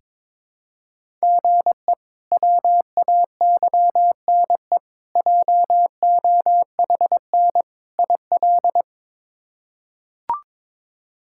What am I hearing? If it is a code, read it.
Morse code, heard as a keyed tone, about 21 wpm, 700 Hz. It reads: GE WAYNE JOHN IL
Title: K1USN SST QSO Practice - 22wpm